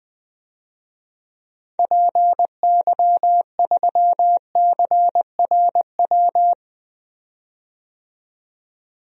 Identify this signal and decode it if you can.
Morse code, heard as a keyed tone, about 20 wpm, 700 Hz. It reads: PY3CRW
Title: DXCC-100 Call Signs - 6 Characters - 20wpm